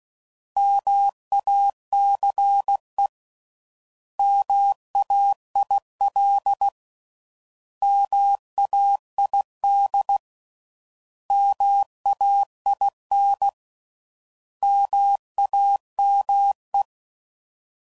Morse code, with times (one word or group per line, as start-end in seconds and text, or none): 0.6-3.1 MACE
4.2-6.7 MAIL
7.8-10.2 MAID
11.3-13.5 MAIN
14.6-16.8 MAME